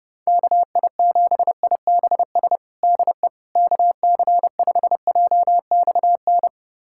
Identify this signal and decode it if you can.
Morse code, heard as a keyed tone, about 30 wpm, 700 Hz. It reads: KI7IBS DE KC5JXN
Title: USA Callsign DE Callsign 30wpm